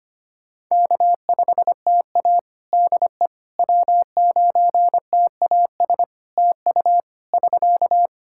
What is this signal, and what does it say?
Morse code, heard as a keyed tone, about 25 wpm, 700 Hz. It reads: K5TA DE W9TAS TU <SK>
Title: Practice QSO - Exchange 4 - Ending - 25wpm